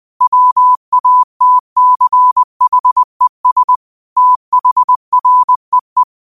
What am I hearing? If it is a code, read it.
Morse code, heard as a keyed tone, about 20 wpm, 1000 Hz. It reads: WATCHES THREE
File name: qotdfiles / 20wpm